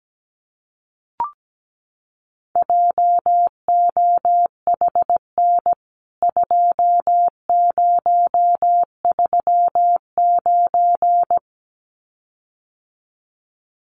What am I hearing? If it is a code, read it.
Morse code, heard as a keyed tone, about 17 wpm, 700 Hz. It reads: JOHN 2039
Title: CWT Exchange 17wpm